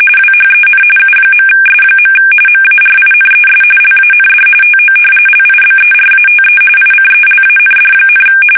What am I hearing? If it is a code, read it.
RTTY, about 45.45 baud. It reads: WHEN U.S. MARINES LAND TO LIBERATE ICELAND,
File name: fsk850 / battleshipnj